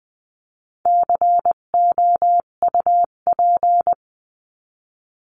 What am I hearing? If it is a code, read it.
Morse code, heard as a keyed tone, about 20 wpm, 700 Hz. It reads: COUP
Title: 258 Lesson 42c - Words 16-20wpm